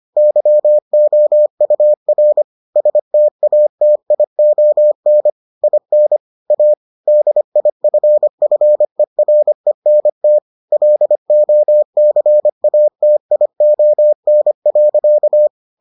Morse code, heard as a keyed tone, about 25 wpm, 600 Hz. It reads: YOUR STATION IN A DIFFERENT LOCATION.